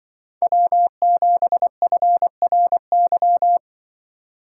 Morse code, heard as a keyed tone, about 23 wpm, 700 Hz. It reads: W7FRY
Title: Speed Racing - USA 1x3 Call Signs 20wpm